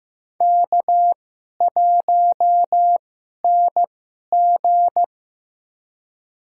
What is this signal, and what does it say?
Morse code, heard as a keyed tone, about 15 wpm, spaced as for 11 wpm, 700 Hz. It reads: K1NG
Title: USA Call Signs - 4 Characters - 2x Character Spacing - 15wpm